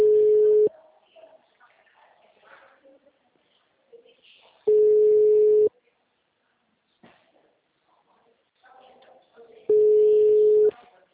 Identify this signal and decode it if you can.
Ringback tone (Europe, 425 Hz).